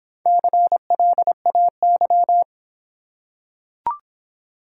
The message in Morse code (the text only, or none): CLAY